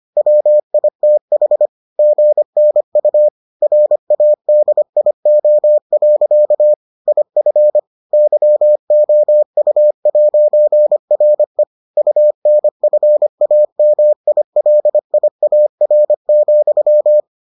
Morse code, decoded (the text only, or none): WITH GNU RADIO. IF YOU'RE UNFAMILIAR,